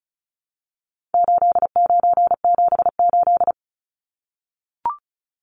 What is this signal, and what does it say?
Morse code, heard as a keyed tone, about 34 wpm, 700 Hz. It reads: 8978
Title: Numbers - 4-Digit 35wpm